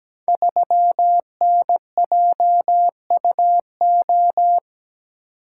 Morse code, heard as a keyed tone, about 17 wpm, 700 Hz. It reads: 3NJUO
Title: Code Groups - 5 Letter:Numbers - 17wpm